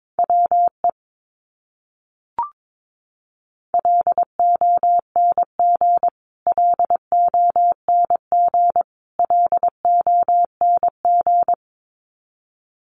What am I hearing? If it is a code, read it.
Morse code, heard as a keyed tone, about 22 wpm, 700 Hz. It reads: WE LONG LONG LONG
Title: Top 200 Words Repeated 22wpm